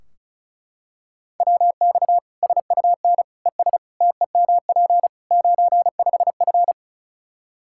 Morse code, heard as a keyed tone, about 36 wpm, 700 Hz. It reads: WX SUN ES TEMP 95F
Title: K1USN SST QSO Practice 35wpm